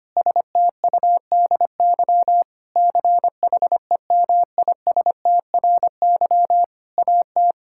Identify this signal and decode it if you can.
Morse code, heard as a keyed tone, about 25 wpm, 700 Hz. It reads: STUDY CHEMISTRY AT
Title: Sentences from Top 1000 Words - Review Part 6 - 25wpm